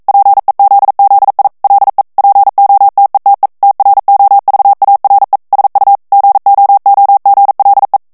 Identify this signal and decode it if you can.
Morse code, heard as a keyed tone, about 42 wpm, 800 Hz. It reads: PEZZI LE POTETE TROVARE SU GOOGLE
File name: morse mp3 / ITALIEN texte clair 1 vitesse 40 mots minute